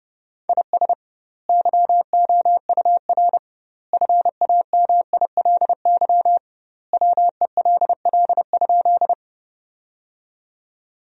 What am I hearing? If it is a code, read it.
Morse code, heard as a keyed tone, about 30 wpm, 700 Hz. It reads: IS YOUR FAMILY WELL?
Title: 2x Word-Spacing - Sentences from Top 400 Words - 30wpm